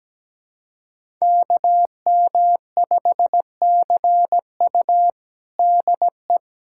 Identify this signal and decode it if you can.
Morse code, heard as a keyed tone, about 17 wpm, 700 Hz. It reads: KM5CU DE
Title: USA Callsign DE Callsign - 5 Characters - 17wpm